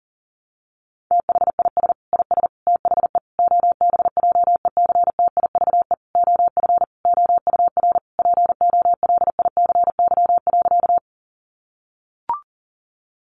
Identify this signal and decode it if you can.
Morse code, heard as a keyed tone, about 37 wpm, 700 Hz. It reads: THIS IS THE OBJECTIVE OF OUR POLICY.
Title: Sentences from Top 1200 Words 40wpm